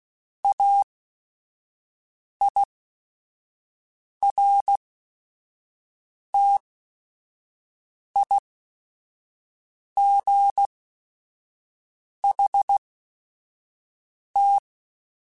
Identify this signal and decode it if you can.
Morse code, heard as a keyed tone, about 16 wpm, spaced as for 5 wpm, 800 Hz. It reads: AIRTIGHT
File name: Chuckles / Chuckles-mp3s / Guaranteed @5